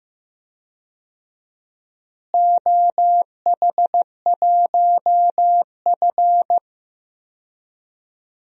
Morse code, heard as a keyed tone, about 15 wpm, 700 Hz. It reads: OH1F
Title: DXCC-100 Call Signs - 4 Characters - Repeated 3x - 15wpm